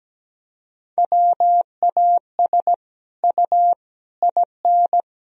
Morse code, heard as a keyed tone, about 17 wpm, 700 Hz. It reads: WAS U IN
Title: Sentences from Top 300 Words and Abbreviations 17wpm